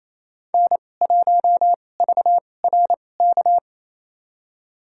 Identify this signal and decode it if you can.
Morse code, heard as a keyed tone, about 28 wpm, spaced as for 20 wpm, 700 Hz. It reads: N1VRK
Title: USA Call Signs - 5 Characters - 2x Character Spacing - 28wpm